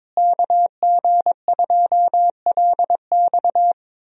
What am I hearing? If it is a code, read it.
Morse code, heard as a keyed tone, about 22 wpm, 700 Hz. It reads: KG2LX
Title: USA Call Signs - 5 Characters - Repeated 3x - 22wpm